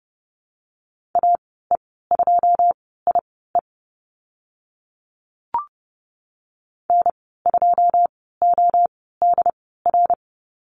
Morse code, heard as a keyed tone, about 30 wpm, spaced as for 17 wpm, 700 Hz. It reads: AE2IE N2ODR
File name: Lesson 14g - Number 2 - Call Signs 30wpm